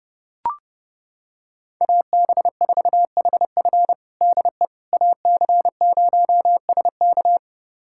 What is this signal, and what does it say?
Morse code, heard as a keyed tone, about 32 wpm, 700 Hz. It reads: AB4HF DE AC0SK
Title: USA Callsign DE Callsign - 5 Characters - 30wpm